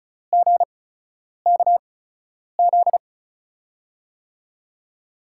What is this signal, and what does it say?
Morse code, heard as a keyed tone, about 36 wpm, spaced as for 10 wpm, 700 Hz. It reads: GKZ
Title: USA Call Signs - 6 Characters - 8x Character Spacing - 35wpm